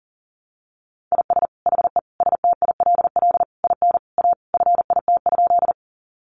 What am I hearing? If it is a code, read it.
Morse code, heard as a keyed tone, about 40 wpm, 700 Hz. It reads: IS HE STILL IN A FIT?
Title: Sentences from Top 800 Words 40wpm